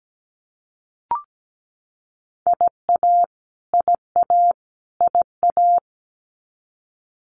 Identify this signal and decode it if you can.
Morse code, heard as a keyed tone, about 17 wpm, 700 Hz. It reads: IA IA IA
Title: US States Repeated 3x 17wpm